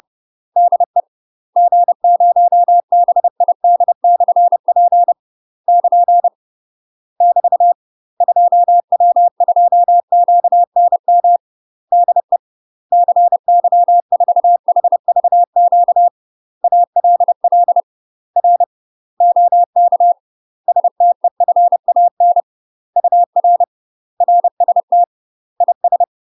Morse code, heard as a keyed tone, about 30 wpm, 700 Hz. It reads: DE G0BID/P <KN> <BT> 2W2QNM DE CY4HVQ ALL R OK STEFAN UR RST IS
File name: UK MASTER QSOs TRAINING / 30WPM / UK MASTER QSOs 0036